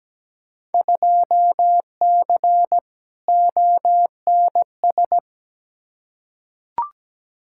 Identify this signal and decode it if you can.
Morse code, heard as a keyed tone, about 17 wpm, 700 Hz. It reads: 2C ONS
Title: Field Day Exchanges Encore - 17wpm